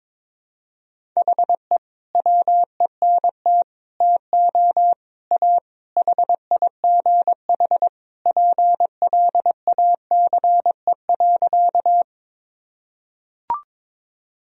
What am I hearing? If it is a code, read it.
Morse code, heard as a keyed tone, about 22 wpm, 700 Hz. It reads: HE WENT TO A HIGH PLACE.
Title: Sentences from Top 1000 Words - Review Part 1 - 22wpm